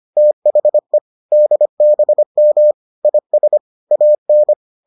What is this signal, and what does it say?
Morse code, heard as a keyed tone, about 25 wpm, 600 Hz. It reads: THE DBM IS AN